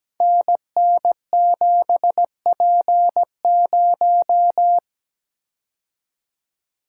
Morse code, heard as a keyed tone, about 17 wpm, 700 Hz. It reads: NN7P0
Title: Code Groups - 5 Letter:Numbers - 17wpm